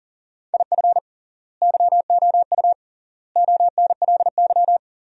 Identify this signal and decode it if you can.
Morse code, heard as a keyed tone, about 40 wpm, 700 Hz. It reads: IF YOU ONLY
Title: 3x Word-Spacing - Sentences from Top 400 Words - 40wpm